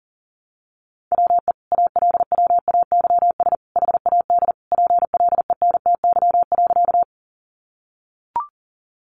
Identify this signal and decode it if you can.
Morse code, heard as a keyed tone, about 40 wpm, 700 Hz. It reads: WE ALWAYS HAD PLENTY.
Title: Sentences from Top 1300 Words 40wpm